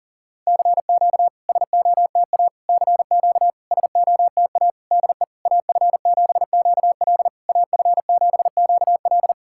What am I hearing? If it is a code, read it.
Morse code, heard as a keyed tone, about 40 wpm, 700 Hz. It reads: CQ SOTA CQ SOTA DE AF7QL AF7QL
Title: SOTA QSO Practice 40wpm